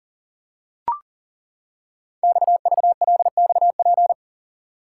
Morse code, heard as a keyed tone, about 42 wpm, 700 Hz. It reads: XVLXP
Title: Code Groups - 5 Letters - 40wpm